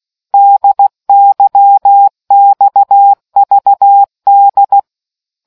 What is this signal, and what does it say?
Morse code, heard as a keyed tone, about 16 wpm, 800 Hz. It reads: DYXVD